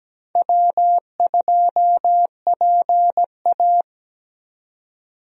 Morse code, heard as a keyed tone, about 17 wpm, 700 Hz. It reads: W2PA
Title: USA Call Signs - 4 Characters - 17wpm